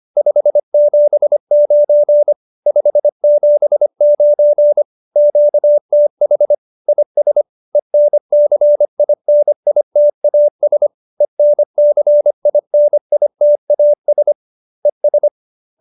Morse code, heard as a keyed tone, about 25 wpm, 590 Hz. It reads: 579 579 QTH IS ENCINITAS ENCINITAS ES